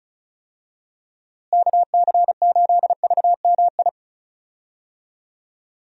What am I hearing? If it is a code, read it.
Morse code, heard as a keyed tone, about 34 wpm, 700 Hz. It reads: KC8VMI
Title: USA Call Signs - 6 Characters - Repeated 3x - 35wpm